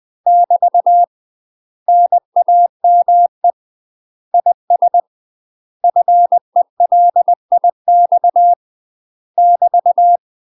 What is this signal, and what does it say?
Morse code, heard as a keyed tone, about 20 wpm, 700 Hz. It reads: <BT> NAME IS FELIX <BT>
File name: UK MASTER QSOs TRAINING / 20WPM / UK MASTER QSOs 0023